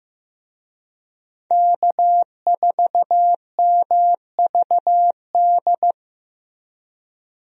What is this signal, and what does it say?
Morse code, heard as a keyed tone, about 15 wpm, 700 Hz. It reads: K4MVD
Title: USA Call Signs - 5 Characters - Repeated 2x - 15wpm